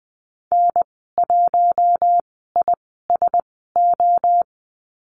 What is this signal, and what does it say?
Morse code, heard as a keyed tone, about 20 wpm, spaced as for 14 wpm, 700 Hz. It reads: N1ISO
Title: 042 Lesson 8g - Number 1 - Call Signs 20wpm